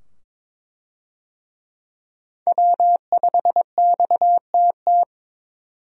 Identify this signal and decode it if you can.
Morse code, heard as a keyed tone, about 21 wpm, 700 Hz. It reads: W5XTT
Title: USA Call Signs - 5 Characters - 22wpm